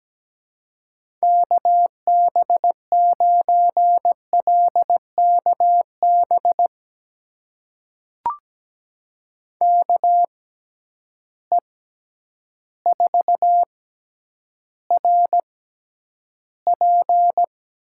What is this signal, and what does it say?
Morse code, heard as a keyed tone, about 17 wpm, spaced as for 9 wpm, 700 Hz. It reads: KB9LKB KE4RP
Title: USA Call Signs - 6 Characters - 6x Character Spacing - 17wpm